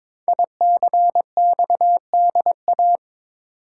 Morse code, heard as a keyed tone, about 22 wpm, 690 Hz. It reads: ICXDA
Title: Code Groups - 5 Letter:Numbers - 22wpm